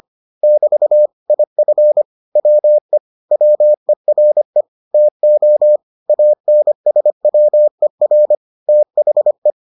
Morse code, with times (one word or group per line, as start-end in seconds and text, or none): 0.4-1.1 <BT>
1.3-2.0 IF
2.4-3.0 WE
3.3-4.6 WERE
4.9-5.8 TO
6.1-8.3 ANSWER
8.7-9.5 THE